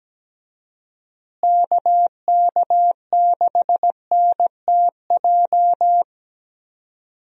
Morse code, heard as a keyed tone, about 17 wpm, 700 Hz. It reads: KK6NTJ
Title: USA Call Signs - 6 Characters - Repeated 3x - 17wpm